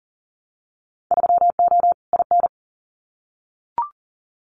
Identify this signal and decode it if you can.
Morse code, heard as a keyed tone, about 39 wpm, 700 Hz. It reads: 3O IN
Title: Winter Field Day Exchanges Encore 40wpm